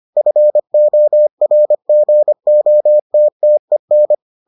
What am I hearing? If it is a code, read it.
Morse code, heard as a keyed tone, about 25 wpm, 600 Hz. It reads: FORGOTTEN